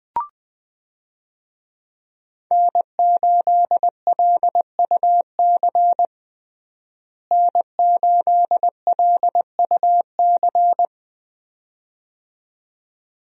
Morse code, heard as a keyed tone, about 20 wpm, 700 Hz. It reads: N8LUC N8LUC
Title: USA Call Signs - 5 Characters - Repeated 2x - 20wpm